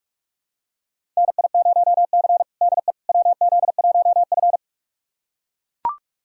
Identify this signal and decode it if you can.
Morse code, heard as a keyed tone, about 42 wpm, 700 Hz. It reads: NI0C DE WZ1F
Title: USA Callsign DE Callsign - 4 Characters - 45wpm